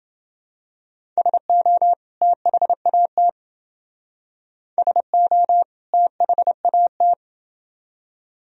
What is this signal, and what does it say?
Morse code, heard as a keyed tone, about 29 wpm, 700 Hz. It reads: SO THAT SO THAT